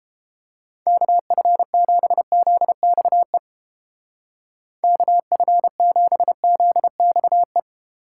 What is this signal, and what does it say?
Morse code, heard as a keyed tone, about 29 wpm, 700 Hz. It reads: KF7ZXE KF7ZXE